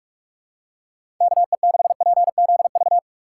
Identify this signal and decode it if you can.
Morse code, heard as a keyed tone, about 44 wpm, 700 Hz. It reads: KE6PZV
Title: USA Call Signs - 6 Characters - 6x Character Spacing - 45wpm